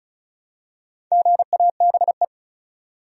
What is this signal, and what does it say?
Morse code, heard as a keyed tone, about 36 wpm, 700 Hz. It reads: GABE
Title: USA Common Names - Max 5 Letters - 35wpm